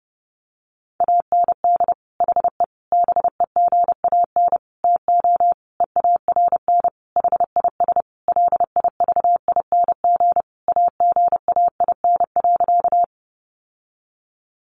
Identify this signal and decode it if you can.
Morse code, heard as a keyed tone, about 30 wpm, 700 Hz. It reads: AND HE BEGAN TO EARN HIS LIVING AGAIN.
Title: Sentences from Top 1500 Words 30wpm